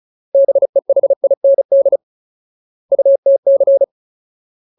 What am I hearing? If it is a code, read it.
Morse code, heard as a keyed tone, about 34 wpm, 550 Hz. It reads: BEHIND UTC
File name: Ultimate Word List / 35WPM Ultimate